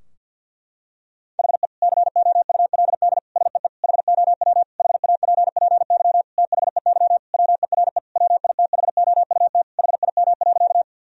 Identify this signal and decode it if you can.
Morse code, heard as a keyed tone, about 52 wpm, 700 Hz. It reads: HE COULD SEE HOW HAPPY THEY WERE WITHOUT HIM.
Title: Sentences from Top 1000 Words - Review Part 3 - 50wpm